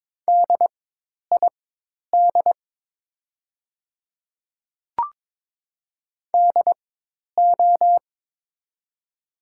Morse code, heard as a keyed tone, about 22 wpm, spaced as for 10 wpm, 700 Hz. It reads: DID DO
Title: 4x Character Spacing - Top 100 Words - 22wpm